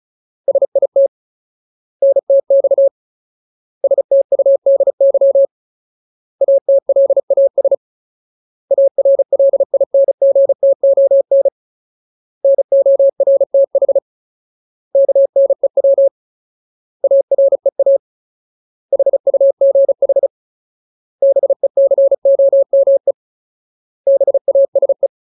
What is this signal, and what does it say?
Morse code, heard as a keyed tone, about 34 wpm, 550 Hz. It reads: SIT NTX STUDY ATLAS ARLINGTON NORTH KNEW AREA HUGH BECOME BASE
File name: Ultimate Word List / 35WPM Ultimate